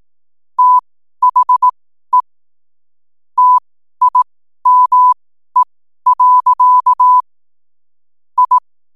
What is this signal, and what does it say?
Morse code, heard as a keyed tone, about 18 wpm, spaced as for 12 wpm, 1000 Hz. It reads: THE TIME. I
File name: n7net / 12wpm